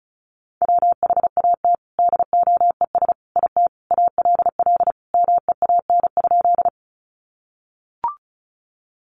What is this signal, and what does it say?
Morse code, heard as a keyed tone, about 34 wpm, 700 Hz. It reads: WHAT DOES IT ALL MEAN?
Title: Sentences from Top 200 Words 35wpm